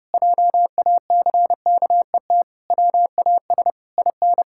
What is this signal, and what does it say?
Morse code, heard as a keyed tone, about 30 wpm, 700 Hz. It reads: JACKET WAS IN
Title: Pangram Challenge v2 - 30wpm